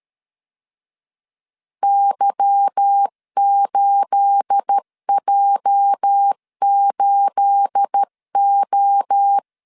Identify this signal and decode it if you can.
Morse code, heard as a keyed tone, about 13 wpm, 800 Hz. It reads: Y8J8O
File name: LEZIONI / 9 28 12wpm 2351